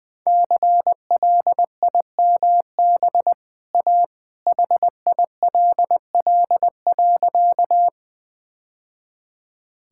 Morse code, heard as a keyed tone, about 20 wpm, 700 Hz. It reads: CLIMB A HILL.
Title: Easy Sentences with Hints - 20wpm